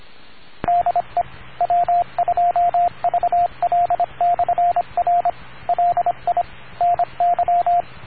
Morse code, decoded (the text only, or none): DE W2VL/R LI NY